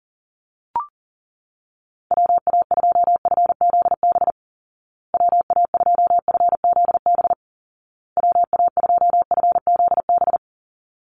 Morse code, heard as a keyed tone, about 40 wpm, 700 Hz. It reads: WA2FZB WA2FZB WA2FZB
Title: Call Signs Repeated 2x - 40wpm